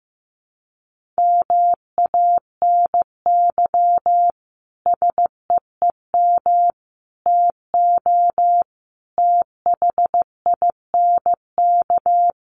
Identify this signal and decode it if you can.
Morse code, heard as a keyed tone, about 14 wpm, 700 Hz. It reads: MANY SEEM TO THINK
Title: Sentences from Top 1400 Words 15wpm